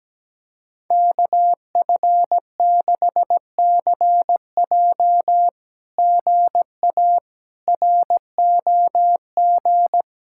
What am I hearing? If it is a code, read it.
Morse code, heard as a keyed tone, about 17 wpm, 700 Hz. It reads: KF6CJ GA ROG
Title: SOTA QSO Practice 17wpm